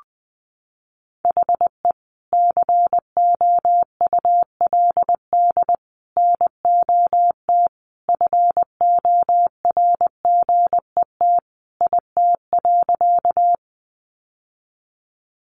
Morse code, heard as a keyed tone, about 20 wpm, 700 Hz. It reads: HE COULD NOT FORGET IT.
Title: Sentences from Top 1200 Words 20wpm